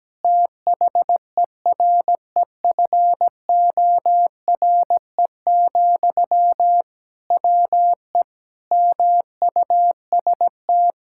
Morse code, heard as a keyed tone, about 17 wpm, 700 Hz. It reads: THEREFORE, WE MUST
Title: Sentences from Top 1000 Words - Review Part 5 - 17wpm